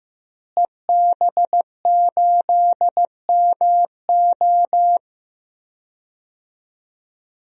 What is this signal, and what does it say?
Morse code, heard as a keyed tone, about 15 wpm, 700 Hz. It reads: EB8MO
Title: DXCC-100 Call Signs - 5 Characters - Repeated 2x - 15wpm